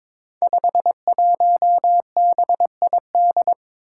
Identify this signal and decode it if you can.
Morse code, heard as a keyed tone, about 23 wpm, 700 Hz. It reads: 51BID
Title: Five Challenging Letter-Number Combinations 22wpm